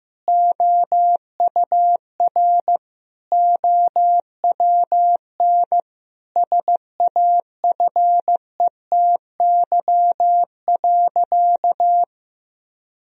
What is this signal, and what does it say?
Morse code, heard as a keyed tone, about 15 wpm, 700 Hz. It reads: OUR OWN SAFETY.
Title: Sentences from Top 1000 Words - Review Part 4 - 15wpm